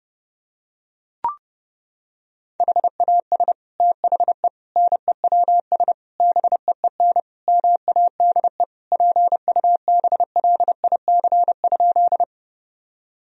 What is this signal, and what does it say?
Morse code, heard as a keyed tone, about 30 wpm, 700 Hz. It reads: HAS THE NEWS BEEN MADE PUBLIC?
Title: Sentences from Top 1000 Words - Review Part 3 - 30wpm